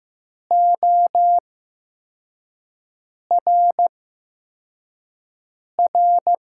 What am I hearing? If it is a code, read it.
Morse code, heard as a keyed tone, about 15 wpm, spaced as for 4 wpm, 700 Hz. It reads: ORR
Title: USA Call Signs - 6 Characters - 8x Character Spacing - 15wpm